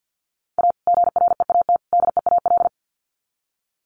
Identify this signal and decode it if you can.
Morse code, heard as a keyed tone, about 50 wpm, 700 Hz. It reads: A GREAT DEAL
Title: Sets of 3 Words 50wpm